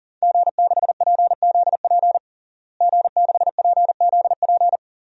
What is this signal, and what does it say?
Morse code, heard as a keyed tone, about 37 wpm, 700 Hz. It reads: G6PZP G6PZP
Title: DXCC-100 Call Signs - 5 Characters - Repeated 2x - 40wpm